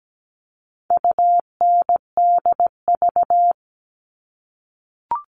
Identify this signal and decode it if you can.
Morse code, heard as a keyed tone, about 17 wpm, 700 Hz. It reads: UNDV
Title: All Cut Numbers - 4-Digit 17wpm